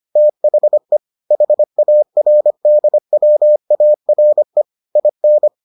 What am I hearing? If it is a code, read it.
Morse code, heard as a keyed tone, about 25 wpm, 600 Hz. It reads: THE HARDWARE IN